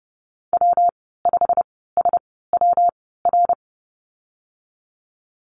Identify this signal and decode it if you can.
Morse code, heard as a keyed tone, about 30 wpm, 700 Hz. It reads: W 5 S W R
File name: Lesson 19g - Letter w - Call Signs 30wpm